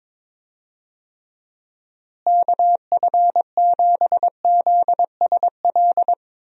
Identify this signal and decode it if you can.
Morse code, heard as a keyed tone, about 22 wpm, 700 Hz. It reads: KF7ZSL
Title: USA Call Signs - 6 Characters - Repeated 2x - 22wpm